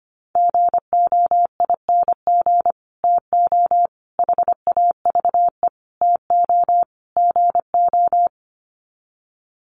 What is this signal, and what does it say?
Morse code, heard as a keyed tone, about 25 wpm, 690 Hz. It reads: GOING TO HAVE TO GO
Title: Sets of 5 Words Encore 25wpm